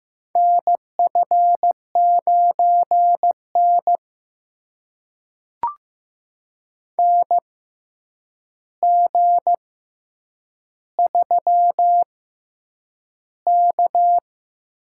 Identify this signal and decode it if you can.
Morse code, heard as a keyed tone, about 15 wpm, spaced as for 8 wpm, 700 Hz. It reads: NF9N NG3K
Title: USA Call Signs - 4 Characters - 6x Character Spacing - 15wpm